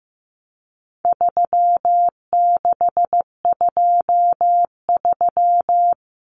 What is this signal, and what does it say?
Morse code, heard as a keyed tone, about 15 wpm, 700 Hz. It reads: 3623
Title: Numbers - 4-Digit 15wpm